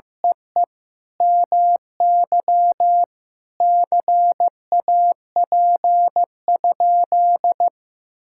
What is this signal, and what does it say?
Morse code, heard as a keyed tone, about 16 wpm, 700 Hz. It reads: EE MY CAP?
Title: Sentences from Top 1600 Words 15wpm